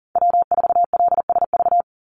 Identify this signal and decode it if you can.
Morse code, heard as a keyed tone, about 40 wpm, 700 Hz. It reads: W4LSV
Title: Call Signs - 40wpm